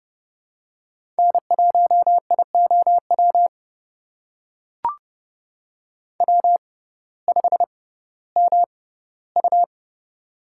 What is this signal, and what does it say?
Morse code, heard as a keyed tone, about 30 wpm, spaced as for 17 wpm, 700 Hz. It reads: N1IOW W5MU
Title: USA Call Signs - 6x Character Spacing - 30wpm